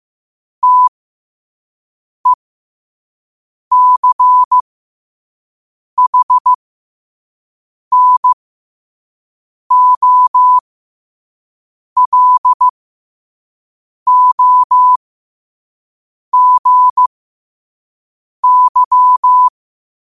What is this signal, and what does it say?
Morse code, heard as a keyed tone, about 15 wpm, spaced as for 5 wpm, 1000 Hz. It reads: TECHNOLOGY
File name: qotdfiles / 5wpm